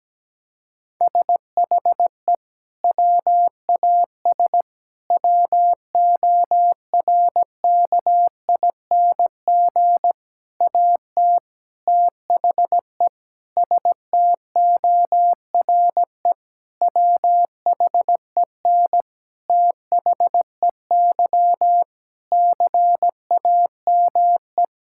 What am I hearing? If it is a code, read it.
Morse code, heard as a keyed tone, about 17 wpm, 700 Hz. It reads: SHE WAS WORKING AT THE STORE WHEN THEY CAME
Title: Sentences from Top 1000 Words - Review Part 5 - 17wpm